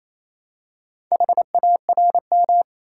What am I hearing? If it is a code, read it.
Morse code, heard as a keyed tone, about 27 wpm, 700 Hz. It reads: HARM
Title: Four Letter Words - 28wpm